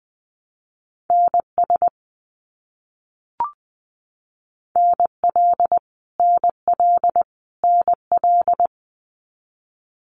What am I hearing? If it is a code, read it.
Morse code, heard as a keyed tone, about 20 wpm, 700 Hz. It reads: NS NL NL NL
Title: Canadian Provinces and Territories - Repeated 3x - 20wpm